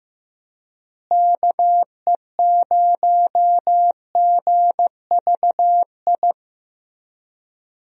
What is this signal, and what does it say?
Morse code, heard as a keyed tone, about 15 wpm, 700 Hz. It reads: KE0GVI
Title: USA Call Signs - 6 Characters - Repeated 2x - 15wpm